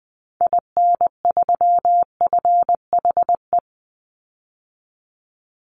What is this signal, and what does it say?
Morse code, heard as a keyed tone, about 19 wpm, 700 Hz. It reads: IN3FHE
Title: Call Signs Repeated 2x - 20wpm